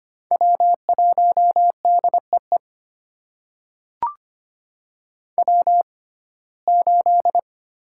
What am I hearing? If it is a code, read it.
Morse code, heard as a keyed tone, about 25 wpm, spaced as for 18 wpm, 700 Hz. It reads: W1DEE W8
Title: USA Call Signs - 5 Characters - 6x Character Spacing - 25wpm